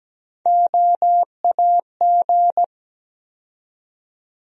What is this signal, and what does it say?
Morse code, heard as a keyed tone, about 17 wpm, 700 Hz. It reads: OAG
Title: USA Call Sign Postfixes - 3 Letters 17wpm